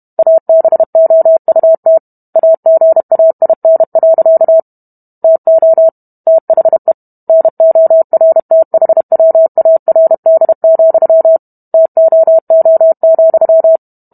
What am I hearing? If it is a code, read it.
Morse code, heard as a keyed tone, about 32 wpm, 650 Hz. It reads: ABOUT AGAIN. TO THE NORTHWARD, TOO,